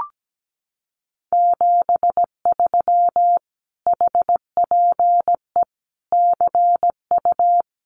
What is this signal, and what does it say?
Morse code, heard as a keyed tone, about 17 wpm, 700 Hz. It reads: 73 HPE CU
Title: Common QSO Phrases Encore 17wpm